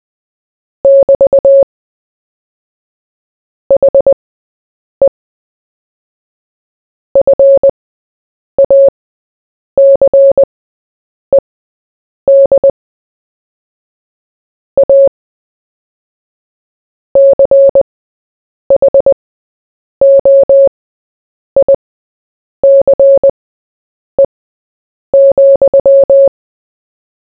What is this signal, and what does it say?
Morse code, heard as a keyed tone, about 20 wpm, spaced as for 8 wpm, 550 Hz. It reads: <BT> HE FACED A CHOICE,
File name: Short-Story-4-8wpm